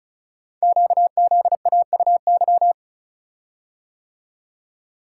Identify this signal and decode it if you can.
Morse code, heard as a keyed tone, about 36 wpm, 690 Hz. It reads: QZAUY
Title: Code Groups - 5 Letters - 35wpm